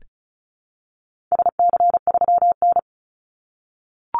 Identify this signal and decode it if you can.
Morse code, heard as a keyed tone, about 34 wpm, 700 Hz. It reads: SC3N
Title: Call Signs - 35wpm